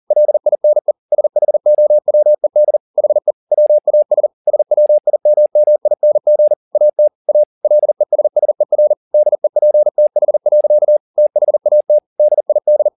Morse code, heard as a keyed tone, about 40 wpm, 600 Hz. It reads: LINE SHOWED HE WAS SWIMMING AT A LESSER DEPTH. THAT DID